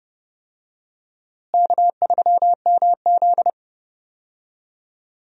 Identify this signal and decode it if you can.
Morse code, heard as a keyed tone, about 30 wpm, 700 Hz. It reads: K3MZ
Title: USA Call Signs - 4 Characters - Repeated 2x - 30wpm